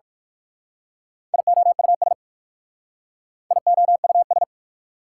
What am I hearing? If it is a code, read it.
Morse code, heard as a keyed tone, about 52 wpm, 690 Hz. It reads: IOUS IOUS
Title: Speed-Racing Prefixes and Suffixes 45wpm